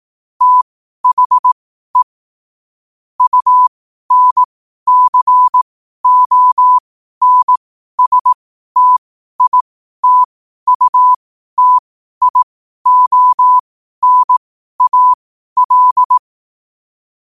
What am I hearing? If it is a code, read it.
Morse code, heard as a keyed tone, about 18 wpm, spaced as for 12 wpm, 1000 Hz. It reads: THE UNCONSTITUTIONAL
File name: qotdfiles / 12wpm